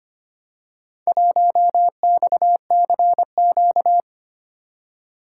Morse code, heard as a keyed tone, about 25 wpm, 700 Hz. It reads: 1XCQ
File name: practice / 25wpm / Four Letter-Number Combinations - 25wpm